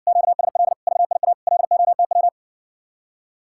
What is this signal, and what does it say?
Morse code, heard as a keyed tone, about 61 wpm, 700 Hz. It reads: QSL VIA LOTW